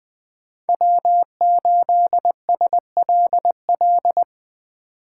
Morse code, heard as a keyed tone, about 21 wpm, 700 Hz. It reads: W8SLL
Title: Speed Racing - USA 1x3 Call Signs 17wpm